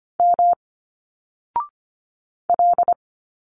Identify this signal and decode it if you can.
Morse code, heard as a keyed tone, about 24 wpm, 700 Hz. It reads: M L
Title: Single Letters 25wpm